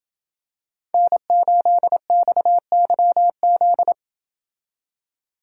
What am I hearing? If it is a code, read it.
Morse code, heard as a keyed tone, about 28 wpm, 700 Hz. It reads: N8XYZ